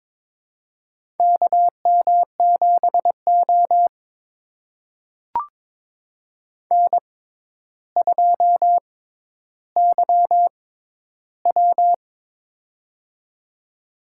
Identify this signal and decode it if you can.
Morse code, heard as a keyed tone, about 22 wpm, spaced as for 11 wpm, 700 Hz. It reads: KM7O N2YW
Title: USA Call Signs - 4 Characters - 6x Character Spacing - 22wpm